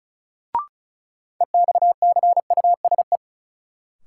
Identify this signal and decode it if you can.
Morse code, heard as a keyed tone, about 36 wpm, 700 Hz. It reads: EXCUSE
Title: Sentences from Top 700 Words - New Word Drill - 35wpm